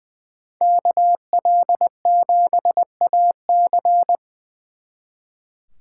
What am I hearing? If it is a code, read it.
Morse code, heard as a keyed tone, about 24 wpm, 700 Hz. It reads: KL7AC
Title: Speed Racing - USA 2x2 Call Signs 20wpm